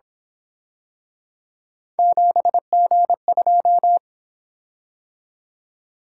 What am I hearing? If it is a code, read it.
Morse code, heard as a keyed tone, about 26 wpm, 700 Hz. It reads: 7G2
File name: Speed-Racing - Three Letter-Number Combinations - 22wpm